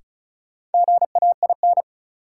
Morse code, heard as a keyed tone, about 36 wpm, 700 Hz. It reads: GAIN
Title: Sentences from Top 1000 Words - New Word Drill - 35wpm